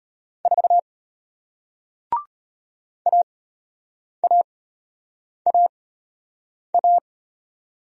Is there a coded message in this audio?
Yes: Morse code from 0.5 to 7.0 s, heard as a keyed tone, about 33 wpm, spaced as for 8 wpm, 700 Hz. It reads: 4 AAAA